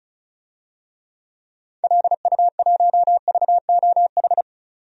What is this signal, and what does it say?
Morse code, heard as a keyed tone, about 34 wpm, 700 Hz. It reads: LU1VOH
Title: DXCC-100 Call Signs - 6 Characters - 35wpm